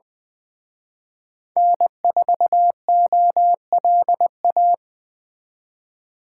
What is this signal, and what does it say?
Morse code, heard as a keyed tone, about 20 wpm, 700 Hz. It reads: N4OLA